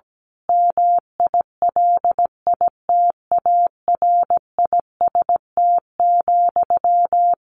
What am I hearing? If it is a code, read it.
Morse code, heard as a keyed tone, about 17 wpm, 700 Hz. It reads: MILITARIST,
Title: A Sound Of Thunder 17wpm